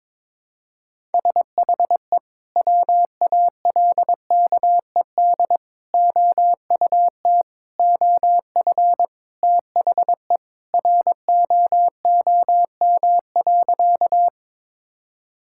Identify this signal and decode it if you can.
Morse code, heard as a keyed tone, about 22 wpm, 700 Hz. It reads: SHE WALKED OUT OF THE ROOM.
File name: practice / 22wpm / Sentences with 3-Word Phrases - 22wpm